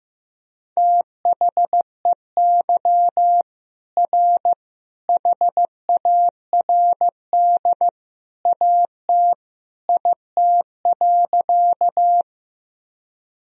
Morse code, heard as a keyed tone, about 15 wpm, 700 Hz. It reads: THEY R HARD AT IT.
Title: Sentences from Top 300 Words and Abbreviations 15wpm